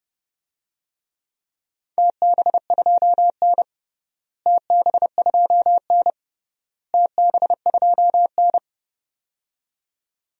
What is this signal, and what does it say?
Morse code, heard as a keyed tone, about 30 wpm, 700 Hz. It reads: TB2N TB2N TB2N
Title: DXCC-100 Call Signs - 4 Characters - Repeated 3x - 30wpm